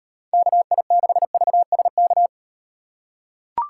Morse code, heard as a keyed tone, about 38 wpm, 700 Hz. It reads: KI6VSK